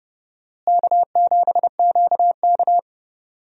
Morse code, heard as a keyed tone, about 31 wpm, 700 Hz. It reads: K7QK
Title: USA Call Signs - 4 Characters - Repeated 2x - 30wpm